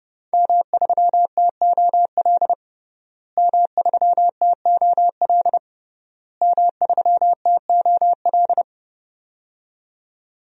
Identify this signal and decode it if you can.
Morse code, heard as a keyed tone, about 30 wpm, 700 Hz. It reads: M3TOL M3TOL M3TOL
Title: DXCC-100 Call Signs - 5 Characters - Repeated 3x - 30wpm